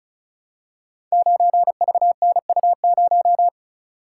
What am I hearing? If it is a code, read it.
Morse code, heard as a keyed tone, about 36 wpm, 700 Hz. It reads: 9VNU0
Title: Code Groups - 5 Letter:Numbers - 35wpm